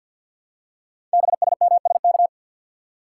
Morse code, heard as a keyed tone, about 52 wpm, 700 Hz. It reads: BSMSK
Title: Code Groups - 5 Letters - 50wpm